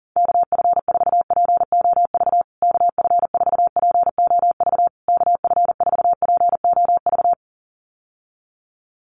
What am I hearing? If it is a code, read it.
Morse code, heard as a keyed tone, about 40 wpm, 700 Hz. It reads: KF4POV KF4POV KF4POV
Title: USA Call Signs Repeated 3x 40wpm